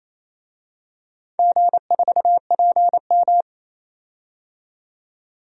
Morse code, heard as a keyed tone, about 29 wpm, 690 Hz. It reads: G4PM
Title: DXCC-100 Call Signs - 4 Characters - 28wpm